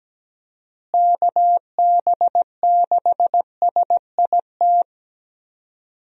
Morse code, heard as a keyed tone, about 17 wpm, 700 Hz. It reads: KB6SIT
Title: USA Call Signs - 6 Characters - Repeated 3x - 17wpm